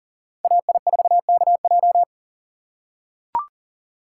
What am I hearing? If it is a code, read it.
Morse code, heard as a keyed tone, about 40 wpm, 700 Hz. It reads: AI4KJ
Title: USA Call Signs - 5 Characters - 6x Character Spacing - 40wpm